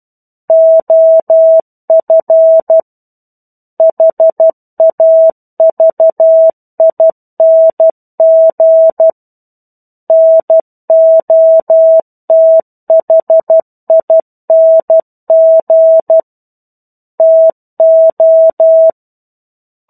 Morse code, heard as a keyed tone, about 12 wpm, 650 Hz. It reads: OF HAVING NOTHING TO
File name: 01-Chapter-01 12 WPM